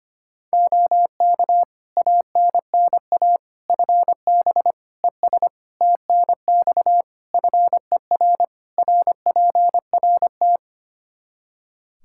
Morse code, heard as a keyed tone, about 25 wpm, 700 Hz. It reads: OK ANNA FB ES TNX FER RPRT
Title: Practice QSO - Exchange 2 - Info - 25wpm